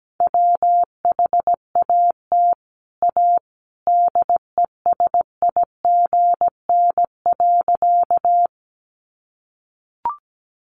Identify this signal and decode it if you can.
Morse code, heard as a keyed tone, about 17 wpm, 700 Hz. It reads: WHAT A DESIGN.
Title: Sentences from Top 800 Words 17wpm